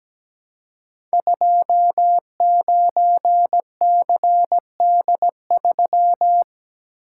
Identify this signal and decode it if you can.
Morse code, heard as a keyed tone, about 17 wpm, 700 Hz. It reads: 29CD3
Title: Code Groups - 5 Letter:Numbers - 17wpm